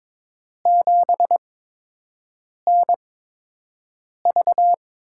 Morse code, heard as a keyed tone, about 22 wpm, spaced as for 6 wpm, 700 Hz. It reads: 7NV
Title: USA Call Signs - 5 Characters - 8x Character Spacing - 22wpm